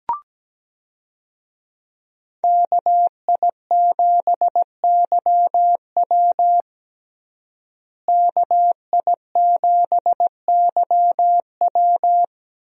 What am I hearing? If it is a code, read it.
Morse code, heard as a keyed tone, about 17 wpm, 700 Hz. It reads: KI7YW KI7YW
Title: USA Call Signs - 5 Characters - Repeated 3x - 17wpm